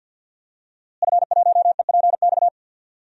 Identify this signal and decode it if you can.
Morse code, heard as a keyed tone, about 51 wpm, 700 Hz. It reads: F1EPX
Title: DXCC-100 Call Signs - 5 Characters - Repeated 2x - 50wpm